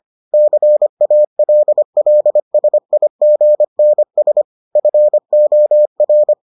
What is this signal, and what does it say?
Morse code, heard as a keyed tone, about 25 wpm, 600 Hz. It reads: CALLSIGNS FOR